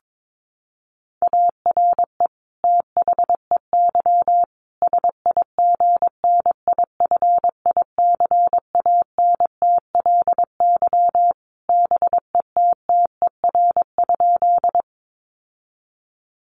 Morse code, heard as a keyed tone, about 22 wpm, 700 Hz. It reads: ARE THEY SIGNIFICANTLY BETTER?
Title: Sentences from Top 1500 Words 22wpm